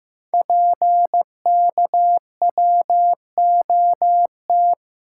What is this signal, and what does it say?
Morse code, heard as a keyed tone, about 15 wpm, 700 Hz. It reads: PKWOT
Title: Code Groups - 5 Letters - 15wpm